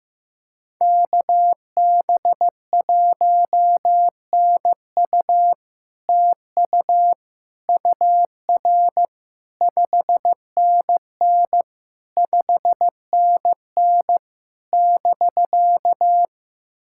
Morse code, heard as a keyed tone, about 15 wpm, 700 Hz. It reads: KB1NU TU UR 5NN 5NN <BK>
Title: SOTA QSO Practice 15wpm